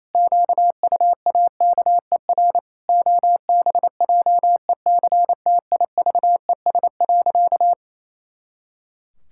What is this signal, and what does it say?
Morse code, heard as a keyed tone, about 28 wpm, 700 Hz. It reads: QUAKER OBJECTIVES.
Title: Pangram Challenge v2 - 28wpm